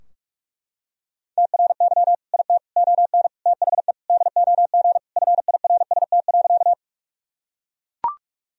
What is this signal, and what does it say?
Morse code, heard as a keyed tone, about 41 wpm, 700 Hz. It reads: TRY IT ON THE DOG FIRST.
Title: Sentences from Top 1000 Words - Review Part 2 - 45wpm